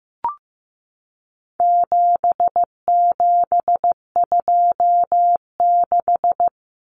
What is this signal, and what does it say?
Morse code, heard as a keyed tone, about 15 wpm, 700 Hz. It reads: 7726
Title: Numbers - 4-Digit 15wpm